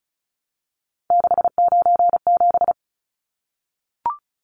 Morse code, heard as a keyed tone, about 34 wpm, 700 Hz. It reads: 697
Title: Numbers - 3-Digit 35wpm